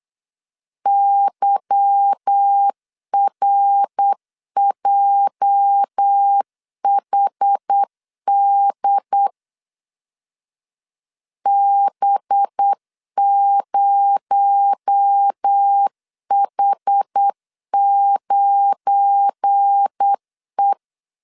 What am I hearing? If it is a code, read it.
Morse code, heard as a keyed tone, about 8 wpm, 800 Hz. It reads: YRJHD B0H9E
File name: LEZIONI / 8 YJ45 / 8 YJ 8wpm 3313